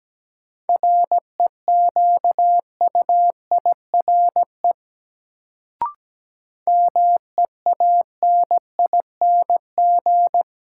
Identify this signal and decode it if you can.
Morse code, heard as a keyed tone, about 17 wpm, 700 Hz. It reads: REQUIRE MEANING
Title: Sentences from Top 900 Words - New Word Drill - 17wpm